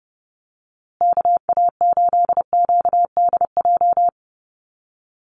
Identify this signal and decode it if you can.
Morse code, heard as a keyed tone, about 30 wpm, 700 Hz. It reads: KA8QDJ
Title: USA Call Signs Repeated 3x 30wpm